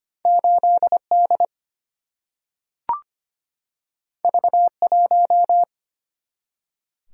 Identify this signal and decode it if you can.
Morse code, heard as a keyed tone, about 25 wpm, 700 Hz. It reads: 8D V1
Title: Single-Letter Single-Number 25wpm